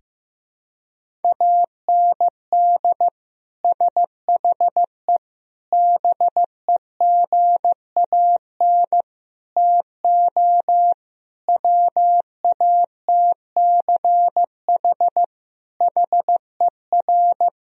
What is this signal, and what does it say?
Morse code, heard as a keyed tone, about 16 wpm, 700 Hz. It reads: AND SHE BEGAN TO WATCH HER
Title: Sentences from Top 1000 Words - Review Part 1 - 15wpm